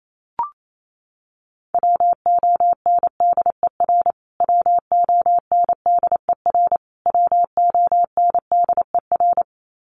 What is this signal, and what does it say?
Morse code, heard as a keyed tone, about 28 wpm, 700 Hz. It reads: WONDER WONDER WONDER
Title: Top 500 Words Repeated 28wpm